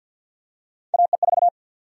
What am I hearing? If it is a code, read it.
Morse code, heard as a keyed tone, about 53 wpm, 700 Hz. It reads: AE4
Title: USA Call Sign Prefixes - 50wpm